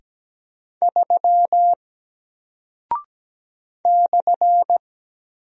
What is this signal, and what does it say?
Morse code, heard as a keyed tone, about 17 wpm, 700 Hz. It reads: 3 /
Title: Single Character (Letter:Number:Punctuation:Prosign) 17wpm